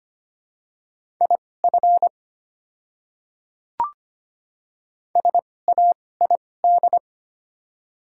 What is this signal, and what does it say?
Morse code, heard as a keyed tone, about 25 wpm, spaced as for 18 wpm, 700 Hz. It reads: IF SAID
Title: 2x Character Spacing - Top 100 Words - 25wpm